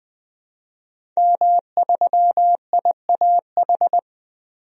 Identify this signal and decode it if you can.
Morse code, heard as a keyed tone, about 20 wpm, 700 Hz. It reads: M3IAH
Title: DXCC-100 Call Signs - 5 Characters - Repeated 2x - 20wpm